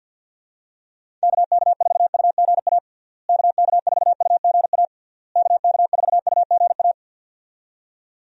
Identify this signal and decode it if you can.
Morse code, heard as a keyed tone, about 50 wpm, 700 Hz. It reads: KK4UGA KK4UGA KK4UGA
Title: USA Call Signs - 6 Characters - Repeated 3x - 50wpm